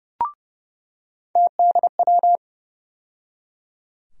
Morse code, heard as a keyed tone, about 29 wpm, 700 Hz. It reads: TDW
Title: USA Call Sign Postfixes - 3 Letters 30wpm